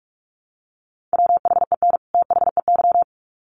Morse code, heard as a keyed tone, about 44 wpm, 700 Hz. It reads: WHEN THEY
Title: Sets of 2 Words 45wpm